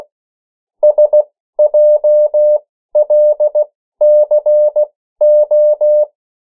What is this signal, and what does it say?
Morse code, heard as a keyed tone, about 16 wpm, spaced as for 13 wpm, 600 Hz. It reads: SJLCO
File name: morse mp3 / 14 urok 13